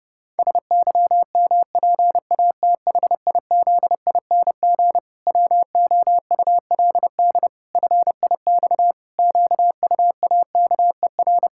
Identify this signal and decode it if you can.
Morse code, heard as a keyed tone, about 32 wpm, 700 Hz. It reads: SYMPATHIZING WOULD FIX QUAKER
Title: Pangram Challenge v2 - 30wpm